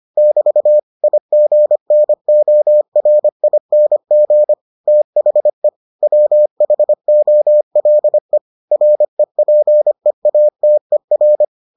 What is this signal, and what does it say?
Morse code, heard as a keyed tone, about 25 wpm, 600 Hz. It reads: <BT> IGNORING THE WHOLE REPEATER